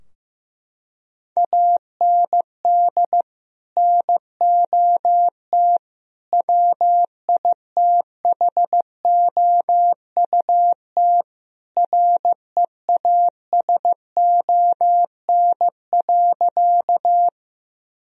Morse code, heard as a keyed tone, about 15 wpm, 700 Hz. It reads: AND NOT WITHOUT REASON.
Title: Sentences from Top 1000 Words - Review Part 3 - 15wpm